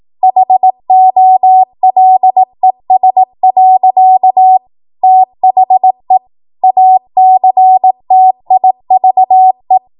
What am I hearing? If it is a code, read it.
Morse code, heard as a keyed tone, about 18 wpm, 750 Hz. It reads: HOLES. THE ACTIVE
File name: Morse / 18WPM